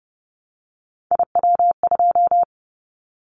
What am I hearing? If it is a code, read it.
Morse code, heard as a keyed tone, about 30 wpm, 700 Hz. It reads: IW2
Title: Callsign Prefixes for Top 25 DXCC - 30wpm